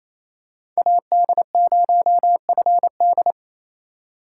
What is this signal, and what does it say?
Morse code, heard as a keyed tone, about 28 wpm, 700 Hz. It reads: AD0FD
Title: USA Call Signs - 5 Characters - 4x Character Spacing - 28wpm